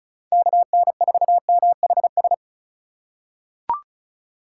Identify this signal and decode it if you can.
Morse code, heard as a keyed tone, about 36 wpm, 700 Hz. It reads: KN4MHS
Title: USA Call Signs - 6 Characters - 3x Character Spacing - 35wpm